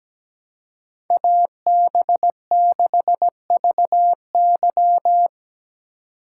Morse code, heard as a keyed tone, about 20 wpm, 700 Hz. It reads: AB6VY
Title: Speed Racing - USA 2x2 Call Signs 17wpm